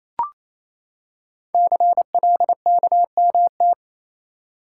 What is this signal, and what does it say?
Morse code, heard as a keyed tone, about 29 wpm, 700 Hz. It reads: CLKMT
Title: Code Groups - 5 Letters - 28wpm